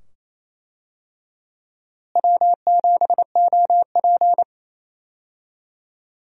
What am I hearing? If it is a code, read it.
Morse code, heard as a keyed tone, about 28 wpm, 700 Hz. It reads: W7OP
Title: USA Call Signs - 4 Characters - Repeated 2x - 28wpm